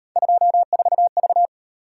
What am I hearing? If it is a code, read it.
Morse code, heard as a keyed tone, about 37 wpm, 700 Hz. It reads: 24V